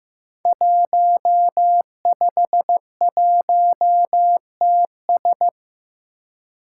Morse code, heard as a keyed tone, about 15 wpm, 700 Hz. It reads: 151TS
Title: Five Challenging Letter-Number Combinations 15wpm